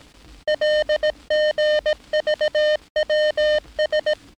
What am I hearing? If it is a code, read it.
Morse code, heard as a keyed tone, about 18 wpm, 610 Hz, with noise. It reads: LGVWS
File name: training / NAVPERS 11049to11060 / NAVPERS 11059 RA RB